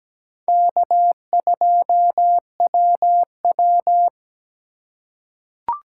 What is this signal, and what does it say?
Morse code, heard as a keyed tone, about 17 wpm, 700 Hz. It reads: K2WW
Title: USA Call Signs - 8x Character Spacing - 17wpm